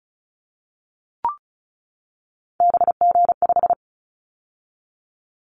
Morse code, heard as a keyed tone, about 36 wpm, 700 Hz. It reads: BG5
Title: Callsign Prefixes for Top 100 DXCC - 35wpm